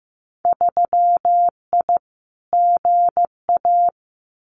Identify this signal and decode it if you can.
Morse code, heard as a keyed tone, about 15 wpm, 700 Hz. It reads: 3I GA
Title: Winter Field Day Exchanges 15wpm